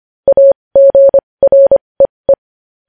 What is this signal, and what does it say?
Morse code, heard as a keyed tone, about 25 wpm, spaced as for 20 wpm, 550 Hz. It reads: AGREE
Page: news headlines in morse code at 20 wpm